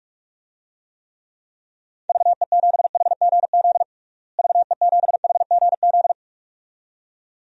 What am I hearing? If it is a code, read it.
Morse code, heard as a keyed tone, about 47 wpm, 700 Hz. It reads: VE7HGZ VE7HGZ
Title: DXCC-100 Call Signs - 6 Characters - Repeated 2x - 45wpm